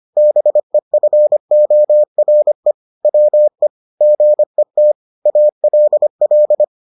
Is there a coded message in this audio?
Yes: Morse code from 0.2 to 6.6 s, heard as a keyed tone, about 25 wpm, 600 Hz. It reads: BEFORE WE GET ALL